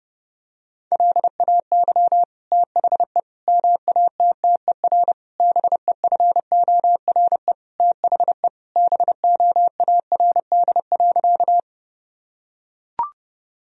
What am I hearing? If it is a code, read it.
Morse code, heard as a keyed tone, about 30 wpm, 700 Hz. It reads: LAY THE MATTER BEFORE THE BOARD.
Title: Sentences from Top 1000 Words - Review Part 4 - 30wpm